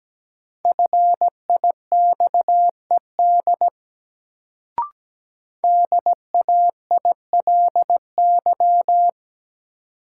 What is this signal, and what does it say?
Morse code, heard as a keyed tone, about 17 wpm, 700 Hz. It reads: FIXED DAILY
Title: Sentences from Top 1100 Words - New Word Drill - 17wpm